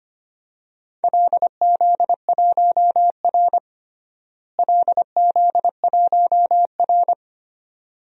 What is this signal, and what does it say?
Morse code, heard as a keyed tone, about 25 wpm, 700 Hz. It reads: LZ1R LZ1R
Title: DXCC-100 Call Signs - 4 Characters - Repeated 2x - 25wpm